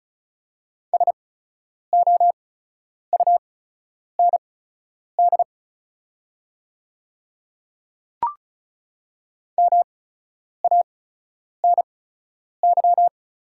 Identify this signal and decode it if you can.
Morse code, heard as a keyed tone, about 36 wpm, spaced as for 10 wpm, 700 Hz. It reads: SOUND MANY
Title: 8x Character Spacing - Top 100 Words - 35wpm